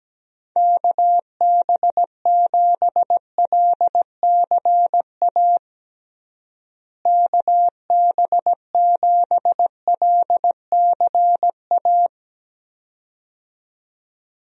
Morse code, heard as a keyed tone, about 17 wpm, 700 Hz. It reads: KB7LCA KB7LCA
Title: USA Call Signs - 6 Characters - Repeated 2x - 17wpm